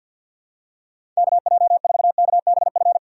Morse code, heard as a keyed tone, about 52 wpm, 700 Hz. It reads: KJ4KBF
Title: USA Call Signs - 6 Characters - 3x Character Spacing - 50wpm